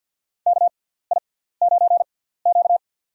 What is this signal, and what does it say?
Morse code, heard as a keyed tone, about 48 wpm, spaced as for 17 wpm, 700 Hz. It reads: KI9Q
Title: USA Call Signs - 4 Characters - 6x Character Spacing - 50wpm